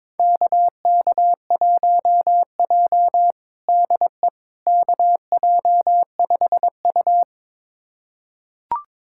Morse code, heard as a keyed tone, about 22 wpm, 700 Hz. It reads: KK1J DE KJ5U
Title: USA Callsign DE Callsign - 4 Characters - 22wpm